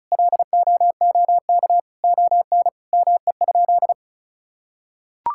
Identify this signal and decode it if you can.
Morse code, heard as a keyed tone, about 36 wpm, 700 Hz. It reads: LOOK ON ME?
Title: Sentences from Top 1000 Words - Review Part 5 - 35wpm